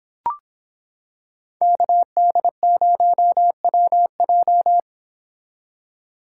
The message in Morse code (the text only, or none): KD0WJ